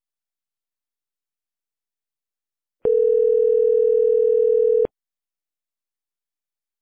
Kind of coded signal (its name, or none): telephone call progress tone